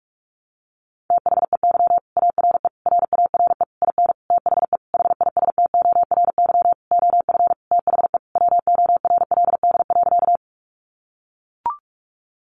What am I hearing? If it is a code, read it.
Morse code, heard as a keyed tone, about 44 wpm, 700 Hz. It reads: THEY ARE RARE IN THE HISTORY OF THE WORLD.
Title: Sentences from Top 1100 Words 45wpm